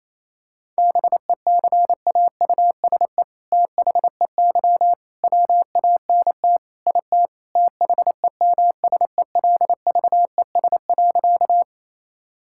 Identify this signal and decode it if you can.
Morse code, heard as a keyed tone, about 28 wpm, 700 Hz. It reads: BECAUSE THEY WANT IT THEMSELVES.
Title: Sentences from Top 1000 Words - Review Part 5 - 28wpm